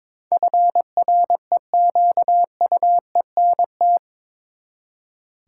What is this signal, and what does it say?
Morse code, heard as a keyed tone, about 21 wpm, 700 Hz. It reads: FREQUENT
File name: practice / 22wpm / Eight Letter Words - 22wpm